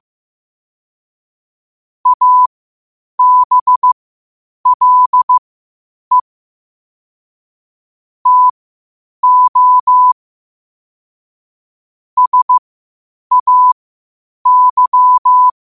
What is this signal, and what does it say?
Morse code, heard as a keyed tone, about 15 wpm, spaced as for 8 wpm, 1000 Hz. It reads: ABLE TO SAY
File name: qotdfiles / 8wpm